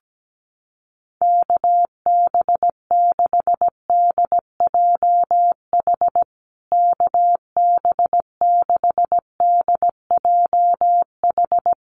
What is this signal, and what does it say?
Morse code, heard as a keyed tone, about 17 wpm, 700 Hz. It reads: KB6DJH KB6DJH
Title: USA Call Signs Repeated 3x 17wpm